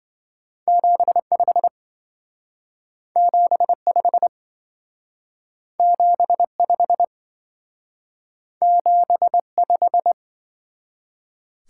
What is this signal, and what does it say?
Morse code, heard as a keyed tone, about 25 wpm, 700 Hz. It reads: 75 75 75 75